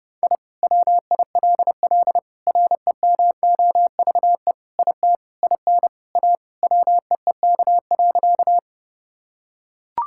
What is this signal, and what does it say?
Morse code, heard as a keyed tone, about 30 wpm, 700 Hz. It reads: I WILL REMOVE IT IN A WEEK.
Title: Sentences from Top 1000 Words - Review Part 5 - 30wpm